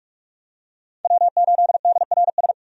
Morse code, heard as a keyed tone, about 43 wpm, 700 Hz. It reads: W8DRS
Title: USA Call Signs - 5 Characters - 45wpm